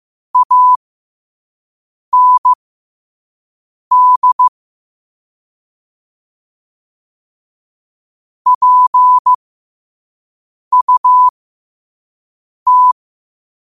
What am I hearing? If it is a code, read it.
Morse code, heard as a keyed tone, about 15 wpm, spaced as for 5 wpm, 1000 Hz. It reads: AND PUT
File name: qotdfiles / 5wpm